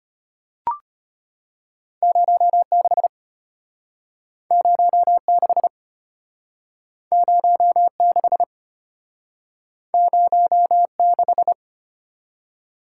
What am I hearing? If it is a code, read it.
Morse code, heard as a keyed tone, about 34 wpm, 700 Hz. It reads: 06 06 06 06